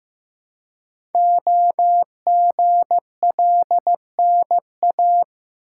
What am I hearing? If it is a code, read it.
Morse code, heard as a keyed tone, about 15 wpm, 700 Hz. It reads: OGLNA
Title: Code Groups - 5 Letters - 15wpm